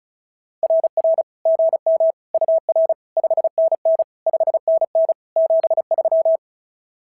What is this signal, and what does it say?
Morse code, heard as a keyed tone, about 36 wpm, 650 Hz. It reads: RR GM UR 5NN 5NN 73
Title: SOTA QSO Practice 35wpm